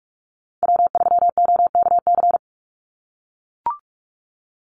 Morse code, heard as a keyed tone, about 44 wpm, 700 Hz. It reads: W3OKC
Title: Call Signs - 45wpm